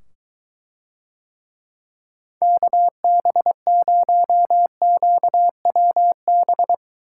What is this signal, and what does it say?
Morse code, heard as a keyed tone, about 23 wpm, 700 Hz. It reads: KB0QWB